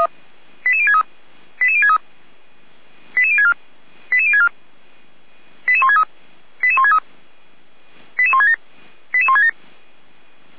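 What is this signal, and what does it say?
Five-tone selective call (ZVEI), heard as quick runs of five tones, tones of about 70 ms. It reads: 88063 88063 88064 88064 88163 88163 88167 88167
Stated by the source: ZVEI five-tone selective call 88367, received with an SDR.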